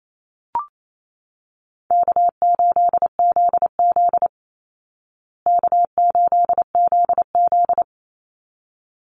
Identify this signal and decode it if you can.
Morse code, heard as a keyed tone, about 28 wpm, 700 Hz. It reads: K8ZZ K8ZZ
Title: Call Signs Repeated 3x - 28wpm